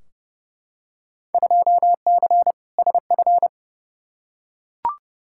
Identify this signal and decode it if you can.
Morse code, heard as a keyed tone, about 30 wpm, 700 Hz. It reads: 2C SF
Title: Field Day Exchanges Encore - 30wpm